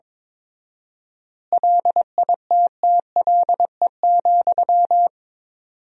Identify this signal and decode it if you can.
Morse code, heard as a keyed tone, about 22 wpm, 700 Hz. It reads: LITTLE,
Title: 4x Word-Spacing - Sentences from Top 400 Words - 22wpm